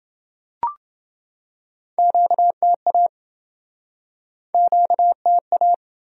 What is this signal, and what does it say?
Morse code, heard as a keyed tone, about 27 wpm, 700 Hz. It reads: QTA QTA